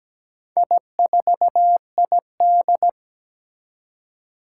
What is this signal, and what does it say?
Morse code, heard as a keyed tone, about 17 wpm, 700 Hz. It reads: I4ID
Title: Four Challenging Letter-Number Combinations 17wpm